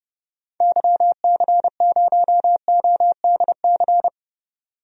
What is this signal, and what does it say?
Morse code, heard as a keyed tone, about 30 wpm, 700 Hz. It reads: YC0ODC
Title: DXCC-100 Call Signs - 6 Characters - 30wpm